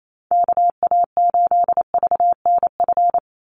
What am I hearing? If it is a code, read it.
Morse code, heard as a keyed tone, about 28 wpm, 700 Hz. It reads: KA8VNF
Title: USA Call Signs 28wpm